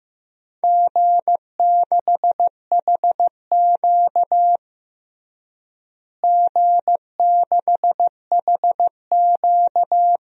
Morse code, heard as a keyed tone, about 15 wpm, 700 Hz. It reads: G6HQ G6HQ
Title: DXCC-100 Call Signs - 4 Characters - Repeated 2x - 15wpm